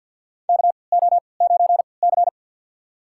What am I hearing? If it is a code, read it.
Morse code, heard as a keyed tone, about 48 wpm, spaced as for 28 wpm, 700 Hz. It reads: KO9C
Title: USA Call Signs - 4 Characters - 3x Character Spacing - 50wpm